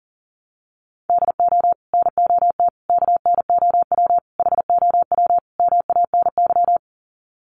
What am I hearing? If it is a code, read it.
Morse code, heard as a keyed tone, about 40 wpm, 700 Hz. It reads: DO NOT KNOW HOW MANY
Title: Sets of 5 Words Encore 40wpm